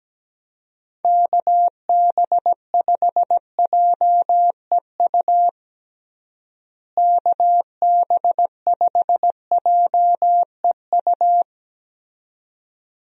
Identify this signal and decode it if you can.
Morse code, heard as a keyed tone, about 17 wpm, 700 Hz. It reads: KB5JEU KB5JEU
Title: USA Call Signs - 6 Characters - Repeated 2x - 17wpm